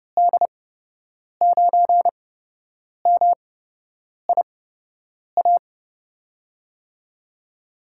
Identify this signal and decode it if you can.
Morse code, heard as a keyed tone, about 32 wpm, spaced as for 8 wpm, 700 Hz. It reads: D9MIA
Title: USA Call Signs - 6 Characters - 8x Character Spacing - 30wpm